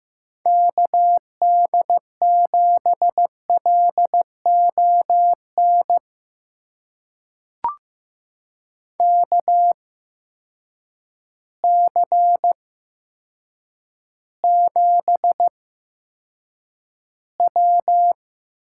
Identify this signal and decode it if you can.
Morse code, heard as a keyed tone, about 15 wpm, spaced as for 8 wpm, 700 Hz. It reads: KD7LON KC7W
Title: USA Call Signs - 6 Characters - 8x Character Spacing - 15wpm